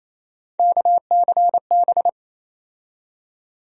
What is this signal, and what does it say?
Morse code, heard as a keyed tone, about 27 wpm, 700 Hz. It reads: KCB
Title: USA Call Sign Postfixes - 3 Letters 28wpm